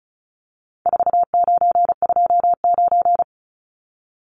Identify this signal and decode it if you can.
Morse code, heard as a keyed tone, about 34 wpm, 700 Hz. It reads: 4929
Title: Numbers - 4-Digit 35wpm